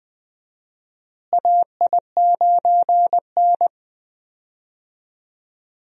Morse code, heard as a keyed tone, about 20 wpm, 700 Hz. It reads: AI9N
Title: USA Call Signs - 4 Characters - Repeated 3x - 20wpm